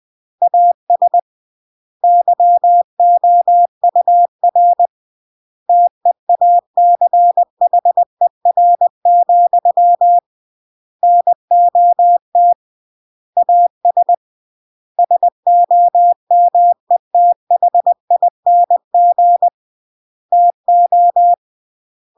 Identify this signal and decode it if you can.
Morse code, heard as a keyed tone, about 20 wpm, 700 Hz. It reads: AS YOUR TEACHER, NOT AS SOMETHING TO
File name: CW / jodorowsky / 82 commandments 20-20